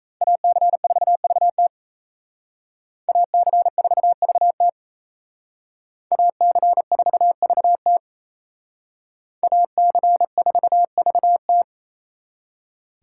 Morse code, heard as a keyed tone, about 35 wpm, 700 Hz. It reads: AC4VT AC4VT AC4VT AC4VT